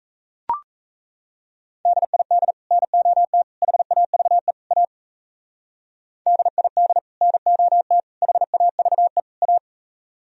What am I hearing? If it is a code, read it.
Morse code, heard as a keyed tone, about 35 wpm, 700 Hz. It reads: DID NOT HAVE A DID NOT HAVE A
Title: Speed-Racing - Sets of 4 Words 28wpm